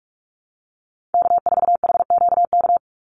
Morse code, heard as a keyed tone, about 44 wpm, 700 Hz. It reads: K4HQK
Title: Call Signs Repeated 2x - 45wpm